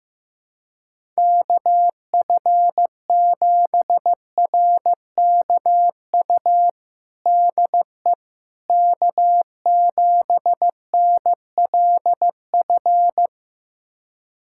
Morse code, heard as a keyed tone, about 15 wpm, 700 Hz. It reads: KF7RKU DE K7NLF
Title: USA Callsign DE Callsign 15wpm